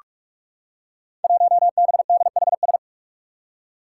Morse code, heard as a keyed tone, about 44 wpm, 700 Hz. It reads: 1BDHS
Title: Five Challenging Letter-Number Combinations 45wpm